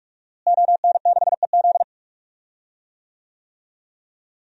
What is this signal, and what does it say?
Morse code, heard as a keyed tone, about 46 wpm, 700 Hz. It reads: ON6EZ
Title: DXCC-100 Call Signs - 5 Characters - Repeated 3x - 45wpm